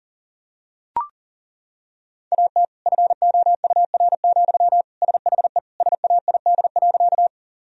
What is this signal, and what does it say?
Morse code, heard as a keyed tone, about 40 wpm, 700 Hz. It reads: AT FOUR, SHE SAID.
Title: Sentences from Top 1000 Words - Review Part 1 - 40wpm